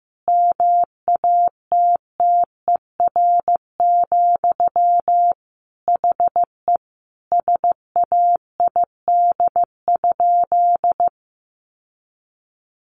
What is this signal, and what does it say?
Morse code, heard as a keyed tone, about 15 wpm, 700 Hz. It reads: MATTER, HE SAID?
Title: Sentences from Top 700 Words 15wpm